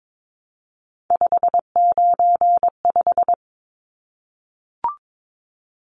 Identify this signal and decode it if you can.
Morse code, heard as a keyed tone, about 22 wpm, 700 Hz. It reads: 595
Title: Numbers - 3-Digit 22wpm